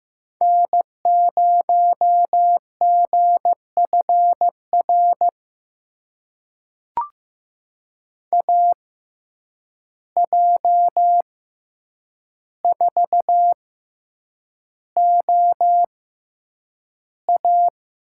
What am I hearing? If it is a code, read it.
Morse code, heard as a keyed tone, about 15 wpm, spaced as for 8 wpm, 700 Hz. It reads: N0GFR AJ4OA
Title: USA Call Signs - 5 Characters - 6x Character Spacing - 15wpm